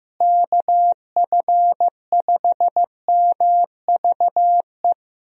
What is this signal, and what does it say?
Morse code, heard as a keyed tone, about 16 wpm, 700 Hz. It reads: KF5MVE
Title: USA Call Signs - 6 Characters - Repeated 3x - 15wpm